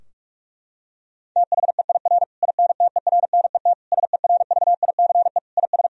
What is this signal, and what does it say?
Morse code, heard as a keyed tone, about 46 wpm, 700 Hz. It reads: THEIR INTERNET SERVICE IS
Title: Sentences from Top 1000 Words - Review Part 4 - 45wpm